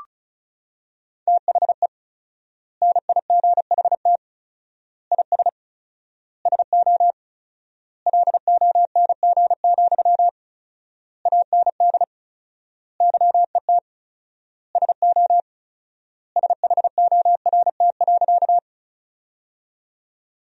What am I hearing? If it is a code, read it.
Morse code, heard as a keyed tone, about 34 wpm, 700 Hz. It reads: THE NIGHT IS SO LONG, AND YET SO SHORT.
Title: 4x Word-Spacing - Sentences from Top 500 Words - 35wpm